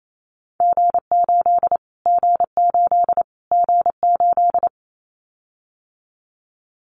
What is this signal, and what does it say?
Morse code, heard as a keyed tone, about 28 wpm, 700 Hz. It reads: G8 G8 G8
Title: Callsign Prefixes for Top 25 DXCC - Repeated 3x - 28wpm